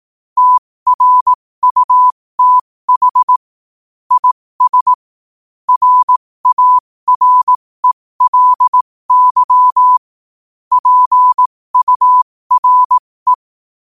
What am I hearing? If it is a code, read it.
Morse code, heard as a keyed tone, about 18 wpm, spaced as for 15 wpm, 1000 Hz. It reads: TRUTH IS RARELY PURE
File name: qotdfiles / 15wpm